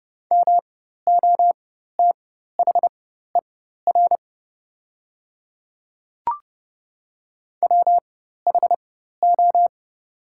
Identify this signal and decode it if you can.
Morse code, heard as a keyed tone, about 30 wpm, spaced as for 14 wpm, 700 Hz. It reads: MOTHER WHO
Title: 4x Character Spacing - Top 200 Words - 30wpm